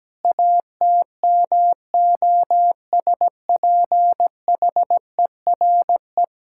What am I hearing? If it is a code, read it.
Morse code, heard as a keyed tone, about 17 wpm, 700 Hz. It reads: ATMOSPHERE
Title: Sentences from Top 1600 Words 17wpm